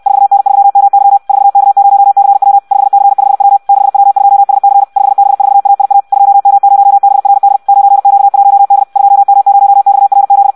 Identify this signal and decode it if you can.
Morse code, heard as a keyed tone, about 64 wpm, 800 Hz. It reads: 4N3MY 4K0CW 5C5W 6W2SC 5B4AIA 2M0BRD 9Q1D 2M0XRZ
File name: Audio practica CW / qrq-tx-66 2